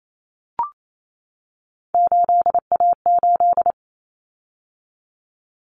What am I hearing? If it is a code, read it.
Morse code, heard as a keyed tone, about 28 wpm, 700 Hz. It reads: 8A8
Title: Common Cut Numbers - 3-Digit 28wpm